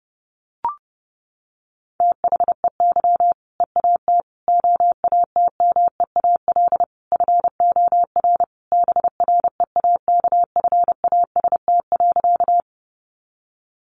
Morse code, heard as a keyed tone, about 30 wpm, 700 Hz. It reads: THEY EAT OATMEAL FOR BREAKFAST.
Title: Easy Phrases 30wpm